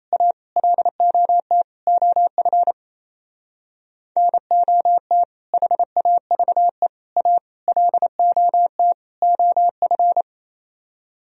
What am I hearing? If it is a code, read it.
Morse code, heard as a keyed tone, about 35 wpm, 700 Hz. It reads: A LOT OF NOT HAVE A LOT OF
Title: Speed-Racing - Sets of 5 Words 28wpm